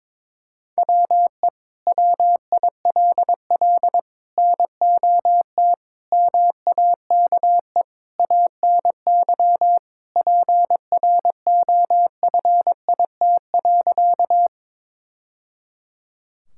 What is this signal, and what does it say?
Morse code, heard as a keyed tone, about 22 wpm, 700 Hz. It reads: WE WILL NOT MAKE ANY PROFIT.
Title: Sentences from Top 1000 Words - Review Part 5 - 22wpm